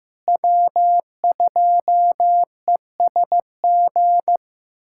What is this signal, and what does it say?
Morse code, heard as a keyed tone, about 15 wpm, 700 Hz. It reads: W2ESG
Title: USA Callsign DE Callsign - 5 Characters - 15wpm